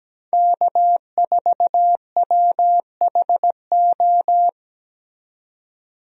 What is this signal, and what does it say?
Morse code, heard as a keyed tone, about 17 wpm, 700 Hz. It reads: K4WHO
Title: USA Call Signs - 5 Characters - 17wpm